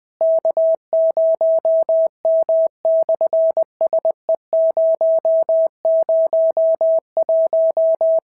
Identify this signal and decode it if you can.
Morse code, heard as a keyed tone, about 20 wpm, 650 Hz. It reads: K0M/SE001
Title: SOTA QSO Practice 20wpm